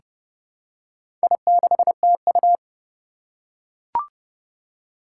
Morse code, heard as a keyed tone, about 30 wpm, 700 Hz. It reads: I6TU
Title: Four Challenging Letter-Number Combinations 30wpm